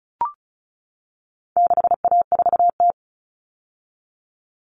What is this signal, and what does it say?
Morse code, heard as a keyed tone, about 34 wpm, 700 Hz. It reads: 6A4T
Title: All Cut Numbers - 4-Digit 35wpm